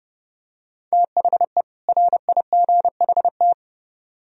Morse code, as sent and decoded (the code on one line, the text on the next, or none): - .... / .-. .. --. .... -
THE RIGHT